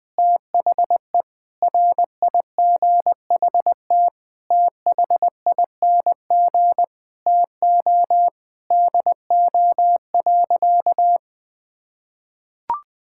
Morse code, heard as a keyed tone, about 20 wpm, 700 Hz. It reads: THE RIGHT THING TO DO.
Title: Sentences with 8-Word Phrases - 20wpm